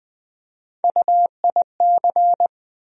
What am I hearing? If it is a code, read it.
Morse code, heard as a keyed tone, about 20 wpm, 700 Hz. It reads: UIC
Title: USA Call Sign Postfixes - 3 Letters 20wpm